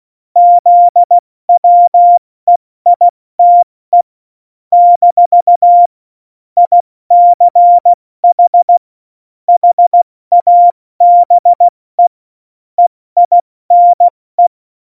Morse code, heard as a keyed tone, about 16 wpm, 700 Hz. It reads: ZWEITE - ICH HABE EINE
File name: Morsekurs / Witze / Witz 29. Dezember 2020, 70 F80